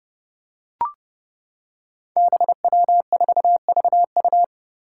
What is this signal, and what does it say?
Morse code, heard as a keyed tone, about 29 wpm, 700 Hz. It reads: BW4VU
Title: Five Challenging Letter-Number Combinations 30wpm